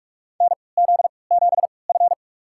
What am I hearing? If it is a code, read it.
Morse code, heard as a keyed tone, about 46 wpm, spaced as for 26 wpm, 700 Hz. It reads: NZ7F
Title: USA Call Signs - 3x Character Spacing - 45wpm